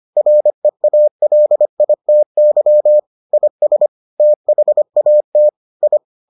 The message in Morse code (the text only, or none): REALITY IS THAT I